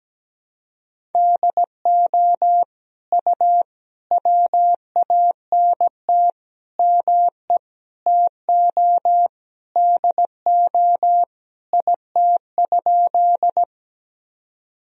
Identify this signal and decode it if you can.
Morse code, heard as a keyed tone, about 17 wpm, 700 Hz. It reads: DO U WANT ME TO DO IT?
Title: Sentences from Top 200 Words and Abbreviations 17wpm